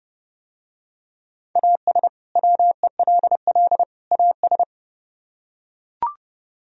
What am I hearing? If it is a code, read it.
Morse code, heard as a keyed tone, about 28 wpm, 700 Hz. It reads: AS WELL AS